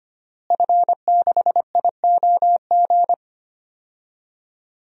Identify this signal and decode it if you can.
Morse code, heard as a keyed tone, about 25 wpm, 700 Hz. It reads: F6IOG
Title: DXCC-100 Call Signs - 5 Characters - Repeated 2x - 25wpm